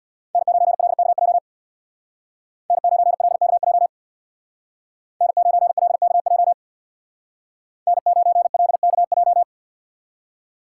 Morse code, heard as a keyed tone, about 64 wpm, 700 Hz. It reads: N9LKJ N9LKJ N9LKJ N9LKJ